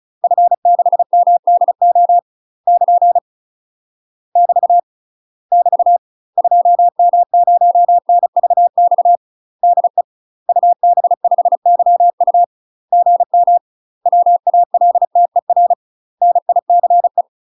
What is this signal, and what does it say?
Morse code, heard as a keyed tone, about 36 wpm, 700 Hz. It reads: F6MDO <KN> <BT> <BT> 2M0NVX DE UB5YU GM WALTER NICE
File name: UK MASTER QSOs TRAINING / 35WPM / UK MASTER QSOs 0031